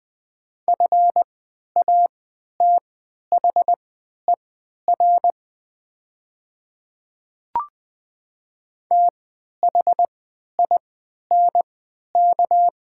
Morse code, heard as a keyed tone, about 20 wpm, spaced as for 11 wpm, 700 Hz. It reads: FATHER THINK
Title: 3x Character Spacing - Top 300 Words - 20wpm